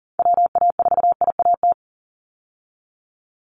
Morse code, heard as a keyed tone, about 40 wpm, 700 Hz. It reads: WA4IAT
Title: USA Call Signs 40wpm